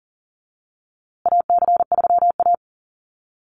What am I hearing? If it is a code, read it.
Morse code, heard as a keyed tone, about 40 wpm, 700 Hz. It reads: AC3A
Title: USA Call Signs Repeated 3x 40wpm